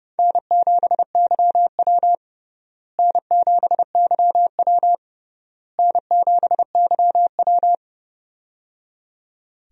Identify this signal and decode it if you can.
Morse code, heard as a keyed tone, about 30 wpm, 700 Hz. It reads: N7YW N7YW N7YW
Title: USA Call Signs - 4 Characters - Repeated 3x - 30wpm